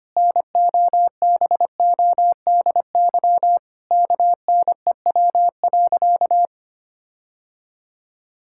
Morse code, heard as a keyed tone, about 25 wpm, 700 Hz. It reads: NOBODY KNEW.
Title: Sentences from Top 1600 Words 25wpm